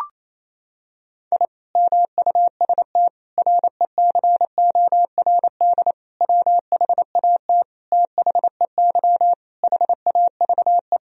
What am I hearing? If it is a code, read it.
Morse code, heard as a keyed tone, about 27 wpm, 700 Hz. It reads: I MUST RECORD WHAT THEY HAVE
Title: Sentences from Top 1000 Words - Review Part 3 - 28wpm